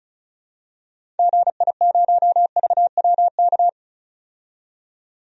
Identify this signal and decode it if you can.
Morse code, heard as a keyed tone, about 34 wpm, 700 Hz. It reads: GI0VWK
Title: DXCC-100 Call Signs - 6 Characters - 35wpm